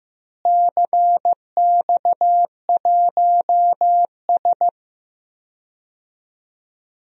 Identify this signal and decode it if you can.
Morse code, heard as a keyed tone, about 15 wpm, 700 Hz. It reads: CX1S
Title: DXCC-100 Call Signs - 4 Characters - Repeated 3x - 15wpm